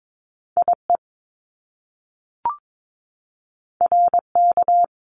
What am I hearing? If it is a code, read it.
Morse code, heard as a keyed tone, about 22 wpm, 700 Hz. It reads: IE RK
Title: Top Two Letter Combinations in Words 22wpm